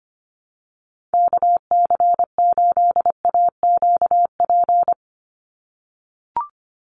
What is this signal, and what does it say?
Morse code, heard as a keyed tone, about 25 wpm, 700 Hz. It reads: KC8AQP
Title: USA Call Signs 25wpm